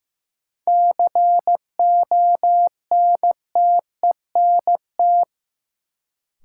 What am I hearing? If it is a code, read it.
Morse code, heard as a keyed tone, about 15 wpm, 700 Hz. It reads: CONTENT
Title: Sentences from Top 900 Words - New Word Drill - 15wpm